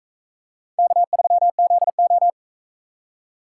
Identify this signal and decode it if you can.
Morse code, heard as a keyed tone, about 44 wpm, 700 Hz. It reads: K3ZO